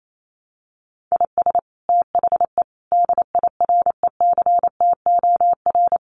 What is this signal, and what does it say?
Morse code, heard as a keyed tone, about 28 wpm, 700 Hz. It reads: IS THE DIRECTOR
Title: Sets of 5 Words Encore2 28wpm